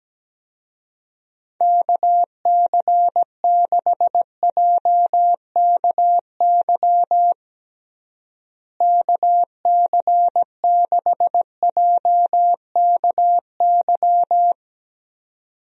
Morse code, heard as a keyed tone, about 17 wpm, 700 Hz. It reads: KC6JKY KC6JKY
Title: USA Call Signs - 6 Characters - Repeated 2x - 17wpm